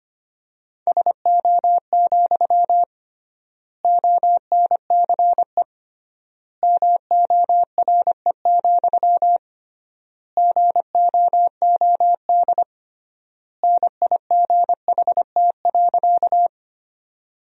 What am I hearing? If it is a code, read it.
Morse code, heard as a keyed tone, about 25 wpm, 700 Hz. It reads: SO, ONCE MORE, GOOD NIGHT.
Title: 3x Word-Spacing - Sentences from Top 300 Words - 25wpm